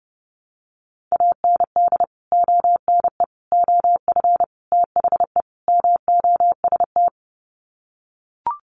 Morse code, heard as a keyed tone, about 30 wpm, 700 Hz. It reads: AND ONE OF THE MOST
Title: Sets of 5 Words Encore 30wpm